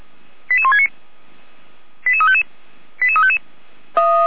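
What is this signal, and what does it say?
Five-tone selective call (ZVEI), heard as quick runs of five tones, tones of about 70 ms. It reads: 88169 88355 88355
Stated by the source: Five-tone ZVEI selective call received by SDR.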